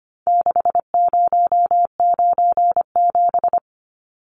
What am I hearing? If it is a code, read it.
Morse code, heard as a keyed tone, about 25 wpm, 700 Hz. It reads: 6097
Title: Numbers - 4-Digit 25wpm